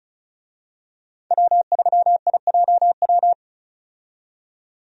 Morse code, heard as a keyed tone, about 36 wpm, 700 Hz. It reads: W3IJW
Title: USA Call Signs - 5 Characters - 35wpm